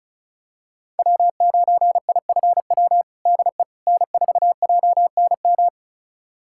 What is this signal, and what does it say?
Morse code, heard as a keyed tone, about 36 wpm, 700 Hz. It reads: W9IFW DE N4JNM
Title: USA Callsign DE Callsign - 5 Characters - 35wpm